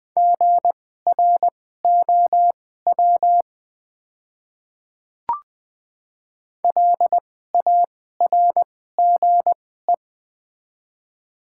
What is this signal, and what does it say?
Morse code, heard as a keyed tone, about 20 wpm, spaced as for 15 wpm, 690 Hz. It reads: GROW LARGE
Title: 2x Character Spacing - Top 300 Words - 20wpm